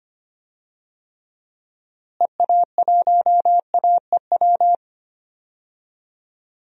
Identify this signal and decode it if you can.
Morse code, heard as a keyed tone, about 25 wpm, 700 Hz. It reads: EA1AEW
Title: DXCC-100 Call Signs - 6 Characters - Repeated 3x - 25wpm